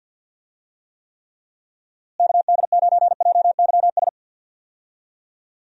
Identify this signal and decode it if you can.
Morse code, heard as a keyed tone, about 50 wpm, 700 Hz. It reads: KD9JYS
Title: USA Call Signs - 6 Characters - Repeated 3x - 50wpm